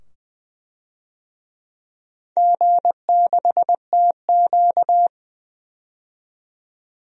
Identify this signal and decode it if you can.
Morse code, heard as a keyed tone, about 20 wpm, 700 Hz. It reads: G6TQ
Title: DXCC-100 Call Signs - 4 Characters - 20wpm